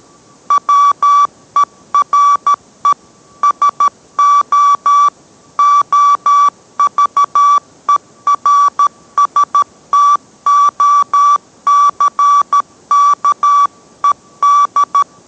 Morse code, heard as a keyed tone, about 13 wpm, 1150 Hz. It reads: WERE SO OVERSTOCKED